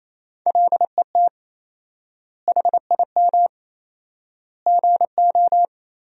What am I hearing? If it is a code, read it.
Morse code, heard as a keyed tone, about 28 wpm, 700 Hz. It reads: LET HIM GO
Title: 4x Word-Spacing - Sentences from Top 500 Words - 28wpm